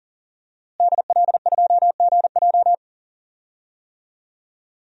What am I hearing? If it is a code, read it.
Morse code, heard as a keyed tone, about 40 wpm, 700 Hz. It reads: DL2GJ
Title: DXCC-100 Call Signs - 5 Characters - Repeated 2x - 40wpm